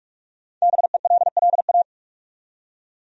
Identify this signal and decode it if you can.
Morse code, heard as a keyed tone, about 46 wpm, 700 Hz. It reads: BELLA
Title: USA Common Names - 45wpm